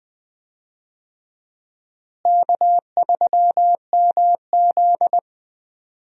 Morse code, heard as a keyed tone, about 20 wpm, 700 Hz. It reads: K3MZ
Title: USA Call Signs - 4 Characters - Repeated 2x - 20wpm